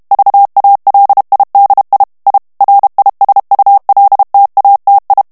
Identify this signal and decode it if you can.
Morse code, heard as a keyed tone, about 32 wpm, 800 Hz. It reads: VALIDI I RISULTATI
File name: texte clair 2 vitesse 30 mots minute